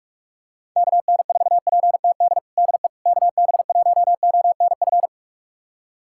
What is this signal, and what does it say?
Morse code, heard as a keyed tone, about 46 wpm, 700 Hz. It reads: KN4PTD DE KB1ONF
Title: USA Callsign DE Callsign - 6 Characters - 45wpm